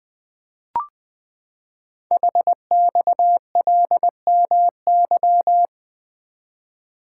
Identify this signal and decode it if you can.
Morse code, heard as a keyed tone, about 20 wpm, 700 Hz. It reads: HXLMY
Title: Code Groups - 5 Letters - 20wpm